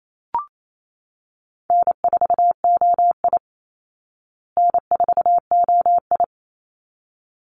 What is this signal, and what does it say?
Morse code, heard as a keyed tone, about 28 wpm, 700 Hz. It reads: N4OI N4OI
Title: Easy Call Signs Repeated 2x - 28wpm